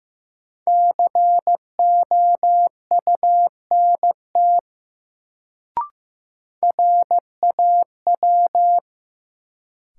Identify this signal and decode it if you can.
Morse code, heard as a keyed tone, about 16 wpm, 700 Hz. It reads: COUNT RAW
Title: Sentences from Top 1500 Words - New Word Drill Alt - 15wpm